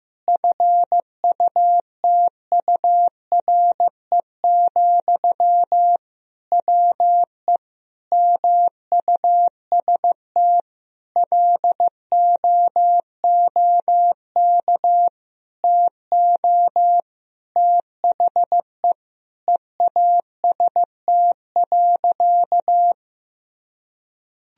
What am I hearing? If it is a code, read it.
Morse code, heard as a keyed tone, about 15 wpm, 700 Hz. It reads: FUTURE, WE MUST LOOK TO THE EAST.
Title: Sentences from Top 1000 Words - Review Part 3 - 15wpm